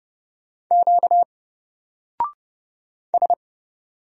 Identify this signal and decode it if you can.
Morse code, heard as a keyed tone, about 29 wpm, 700 Hz. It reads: Q S
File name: Single-Letter-Number-30wpm